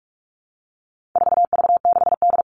Morse code, heard as a keyed tone, about 44 wpm, 700 Hz. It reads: 4V6D
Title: All Cut Numbers - 4-Digit 45wpm